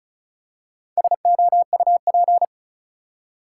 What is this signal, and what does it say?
Morse code, heard as a keyed tone, about 34 wpm, 700 Hz. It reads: SOUP
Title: Four Letter Words - 35wpm